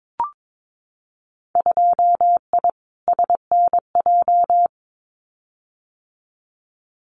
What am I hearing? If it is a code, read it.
Morse code, heard as a keyed tone, about 22 wpm, 700 Hz. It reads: 2I SNJ
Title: Winter Field Day Exchanges Encore 22wpm